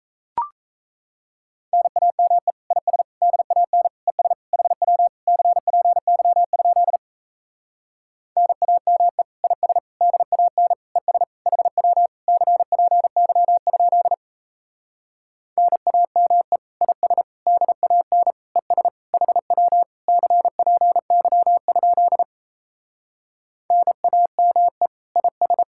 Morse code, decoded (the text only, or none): NAME IS DAN ES HW CPY? NAME IS DAN ES HW CPY? NAME IS DAN ES HW CPY? NAME IS